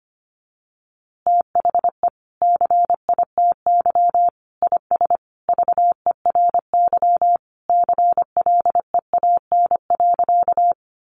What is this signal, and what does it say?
Morse code, heard as a keyed tone, about 25 wpm, 700 Hz. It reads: THE CITY IS VERY CLEAN.
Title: Easy Phrases 25wpm